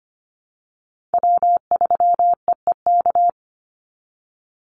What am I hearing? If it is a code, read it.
Morse code, heard as a keyed tone, about 25 wpm, 700 Hz. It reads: W3EEK
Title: Call Signs Repeated 3x - 25wpm